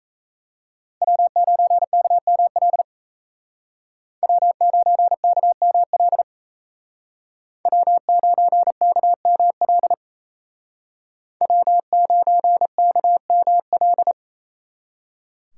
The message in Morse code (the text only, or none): W9KML W9KML W9KML W9KML